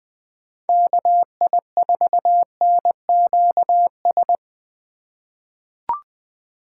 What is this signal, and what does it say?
Morse code, heard as a keyed tone, about 20 wpm, 700 Hz. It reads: KI4NQS
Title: USA Call Signs - 3x Character Spacing - 20wpm